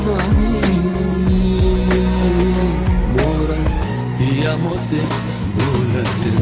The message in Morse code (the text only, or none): none